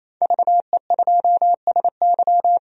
Morse code, heard as a keyed tone, about 28 wpm, 700 Hz. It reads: VE2SY
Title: DXCC-100 Call Signs - 5 Characters - 28wpm